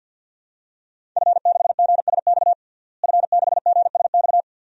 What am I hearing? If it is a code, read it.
Morse code, heard as a keyed tone, about 50 wpm, 700 Hz. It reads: F6GSX F6GSX
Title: DXCC-100 Call Signs - 5 Characters - Repeated 2x - 50wpm